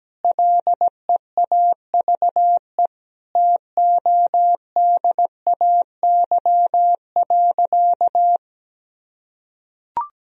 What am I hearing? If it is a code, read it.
Morse code, heard as a keyed tone, about 17 wpm, 700 Hz. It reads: LEAVE TODAY.
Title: Sentences from Top 1000 Words - Review Part 3 - 17wpm